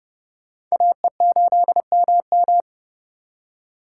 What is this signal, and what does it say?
Morse code, heard as a keyed tone, about 37 wpm, 700 Hz. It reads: AE8MM
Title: Speed Racing - USA 2x2 Call Signs 30wpm